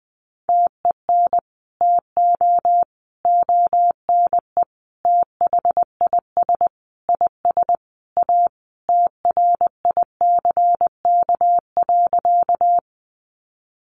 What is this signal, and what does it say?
Morse code, heard as a keyed tone, about 19 wpm, 700 Hz. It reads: TEN TO ONE THIS IS A TRICK.
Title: Sentences from Top 1500 Words 20wpm